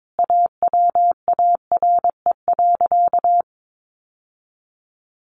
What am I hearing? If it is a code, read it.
Morse code, heard as a keyed tone, about 22 wpm, 700 Hz. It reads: AWARE.
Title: Sentences from Top 1300 Words 22wpm